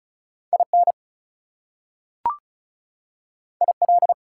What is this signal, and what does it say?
Morse code, heard as a keyed tone, about 36 wpm, 700 Hz. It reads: IN IL
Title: US States and Canadian Provinces 35wpm